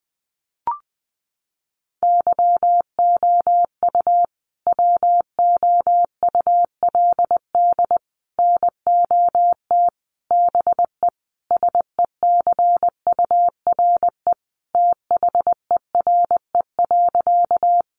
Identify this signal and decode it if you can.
Morse code, heard as a keyed tone, about 20 wpm, 700 Hz. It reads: YOU WOULD NOT BE SECURE THERE.
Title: Sentences from Top 1200 Words 20wpm